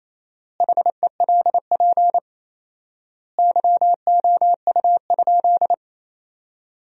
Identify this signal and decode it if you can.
Morse code, heard as a keyed tone, about 28 wpm, 700 Hz. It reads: HELP YOU?
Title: 4x Word-Spacing - Sentences from Top 200 Words - 28wpm